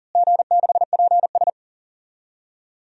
Morse code, heard as a keyed tone, about 40 wpm, 700 Hz. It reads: G6PS
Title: DXCC-100 Call Signs - 4 Characters - Repeated 3x - 40wpm